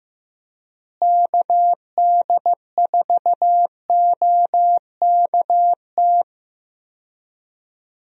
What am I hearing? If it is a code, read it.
Morse code, heard as a keyed tone, about 16 wpm, 700 Hz. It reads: KD4OKT
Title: USA Call Signs - 6 Characters - Repeated 3x - 15wpm